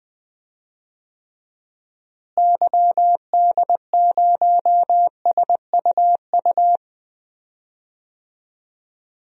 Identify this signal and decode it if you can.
Morse code, heard as a keyed tone, about 20 wpm, 700 Hz. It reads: YD0SUU
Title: DXCC-100 Call Signs - 6 Characters - Repeated 2x - 20wpm